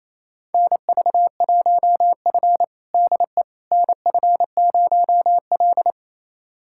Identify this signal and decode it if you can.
Morse code, heard as a keyed tone, about 28 wpm, 700 Hz. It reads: NV1F DE NF0L
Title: USA Callsign DE Callsign - 4 Characters - 28wpm